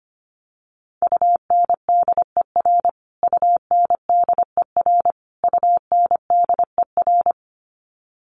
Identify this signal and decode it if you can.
Morse code, heard as a keyed tone, about 25 wpm, 700 Hz. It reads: UNDER UNDER UNDER
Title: Top 300 Words Repeated 25wpm